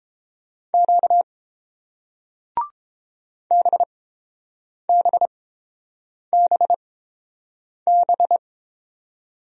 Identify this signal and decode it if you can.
Morse code, heard as a keyed tone, about 30 wpm, spaced as for 7 wpm, 700 Hz. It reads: Q BBBB